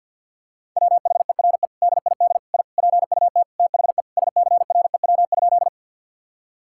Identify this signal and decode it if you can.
Morse code, heard as a keyed tone, about 50 wpm, 700 Hz. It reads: WHERE DID I PUT THE SCREW?
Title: Sentences from Top 1600 Words 50wpm